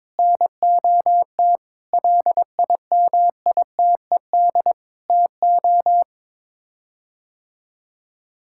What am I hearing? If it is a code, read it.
Morse code, heard as a keyed tone, about 22 wpm, 700 Hz. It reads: NOT LIMITED TO
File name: practice / 22wpm / Sets of 6 Words - 22wpm